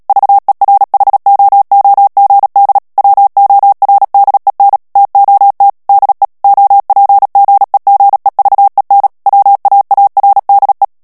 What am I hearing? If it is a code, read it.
Morse code, heard as a keyed tone, about 37 wpm, 800 Hz. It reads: VERHOOGD WORDEN TOT DE OPGEGEVEN WAARDE
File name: morse mp3 / HOLLANDAIS texte clair 2 vitesse 35 mots minute